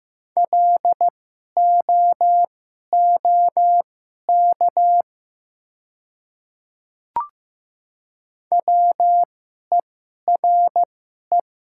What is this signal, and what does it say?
Morse code, heard as a keyed tone, about 15 wpm, spaced as for 11 wpm, 700 Hz. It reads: LOOK WERE
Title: 2x Character Spacing - Top 200 Words - 15wpm